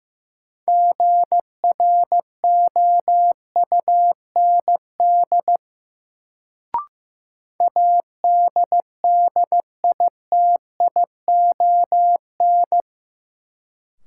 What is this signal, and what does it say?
Morse code, heard as a keyed tone, about 15 wpm, 700 Hz. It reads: GROUND ADDITION
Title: Sets of 3 Words - New Word Drill - 15wpm